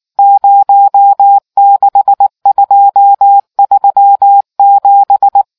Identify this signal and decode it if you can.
Morse code, heard as a keyed tone, about 19 wpm, 800 Hz. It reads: 06237